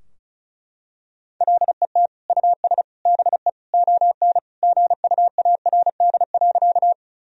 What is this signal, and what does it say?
Morse code, heard as a keyed tone, about 34 wpm, 700 Hz. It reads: LET US BE ON GUARD.
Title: Sentences from Top 1000 Words - Review Part 4 - 35wpm